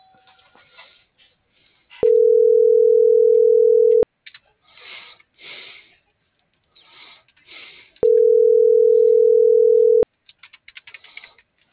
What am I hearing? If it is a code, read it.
Ringback tone (North America), 440 + 480 Hz.